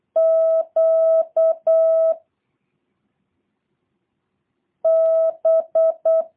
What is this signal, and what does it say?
Morse code, heard as a keyed tone, about 8 wpm, 650 Hz. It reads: Q B
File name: jh8cbh / 2023cw / review1-6